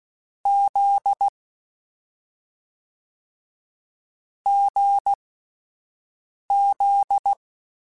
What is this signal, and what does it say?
Morse code, heard as a keyed tone, about 16 wpm, spaced as for 5 wpm, 800 Hz. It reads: Z GZ
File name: Difficult Combos / Difficult Combos-mp3s / awj1 ndb6 gz7 @5